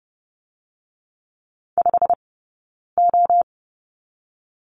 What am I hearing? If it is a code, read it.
Morse code, heard as a keyed tone, about 30 wpm, 700 Hz. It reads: 5 O
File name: course / 30:10wpm / 086 Lesson 15d - Number 5 - Two Characters 30wpm